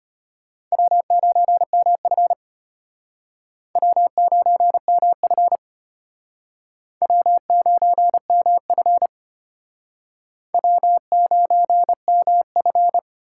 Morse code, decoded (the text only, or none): W9MF W9MF W9MF W9MF